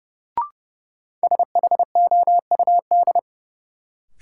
Morse code, heard as a keyed tone, about 30 wpm, 700 Hz. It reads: SHOUD
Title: Sentences from Top 1000 Words - Review Part 4 - New Word Drill - 30wpm